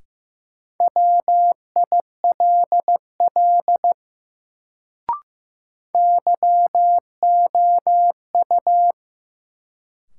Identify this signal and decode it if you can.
Morse code, heard as a keyed tone, about 16 wpm, 700 Hz. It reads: WILL YOU
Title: Sentences from Top 100 Words - New Word Drill - 15wpm